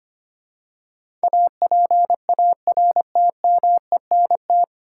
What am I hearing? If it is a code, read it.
Morse code, heard as a keyed tone, about 25 wpm, 700 Hz. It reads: APARTMENT
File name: practice / 25wpm / 9 Letter Words - First 3 Letters Given - 25wpm